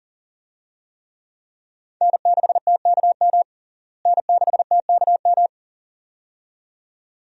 Morse code, heard as a keyed tone, about 40 wpm, 700 Hz. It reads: N6TKM N6TKM
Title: USA Call Signs - 5 Characters - Repeated 2x - 40wpm